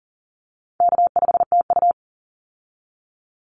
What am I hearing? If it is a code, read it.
Morse code, heard as a keyed tone, about 40 wpm, 700 Hz. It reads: K5TU
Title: Call Signs Repeated 3x - 40wpm